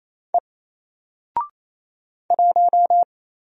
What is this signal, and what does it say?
Morse code, heard as a keyed tone, about 27 wpm, 700 Hz. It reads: E 1
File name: Single-Letter-Number-28wpm